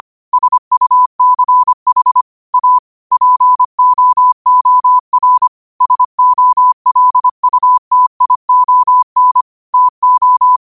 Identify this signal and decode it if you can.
Morse code, heard as a keyed tone, about 25 wpm, 1000 Hz. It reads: SUCH A POOR SOLUTION TO
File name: qotdfiles / 25wpm